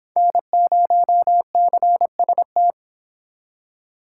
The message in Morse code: N0CST

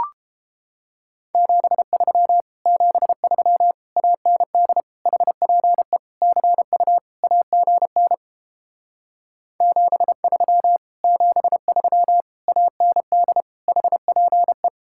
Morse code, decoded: 73 73 AND HPE CU AGN 73 73 AND HPE

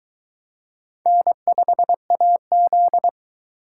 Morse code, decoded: N5AZ

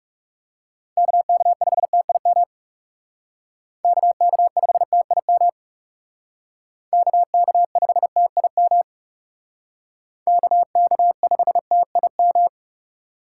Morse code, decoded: KK5TIM KK5TIM KK5TIM KK5TIM

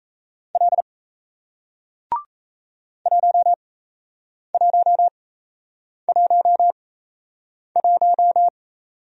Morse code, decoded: L 1111

